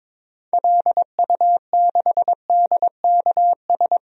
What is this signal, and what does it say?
Morse code, heard as a keyed tone, about 22 wpm, 700 Hz. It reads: LU6DKS
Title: DXCC-100 Call Signs - 6 Characters - Repeated 3x - 22wpm